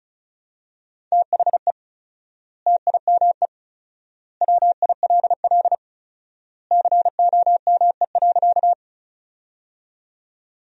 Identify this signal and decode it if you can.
Morse code, heard as a keyed tone, about 36 wpm, 700 Hz. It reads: THE TIME WILL COME.
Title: 4x Word-Spacing - Sentences from Top 100 Words - 35wpm